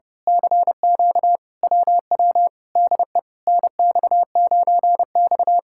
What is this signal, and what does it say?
Morse code, heard as a keyed tone, about 28 wpm, 700 Hz. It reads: CQ WW DE NX9X
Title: Speed-Racing Common QSO Phrases - 30wpm